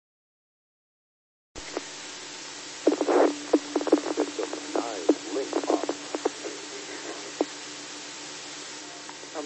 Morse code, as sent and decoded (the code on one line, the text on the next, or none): none